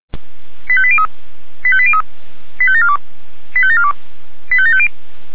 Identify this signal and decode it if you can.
Five-tone selective call (ZVEI), heard as quick runs of five tones, tones of about 70 ms. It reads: 85803 85803 85742 85742 85750
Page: SDR/ZVEI